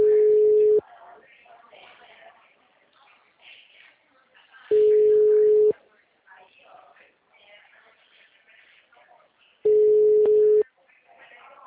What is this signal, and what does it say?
Ringback tone (Europe, 425 Hz), with speech.
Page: Cordeiropolis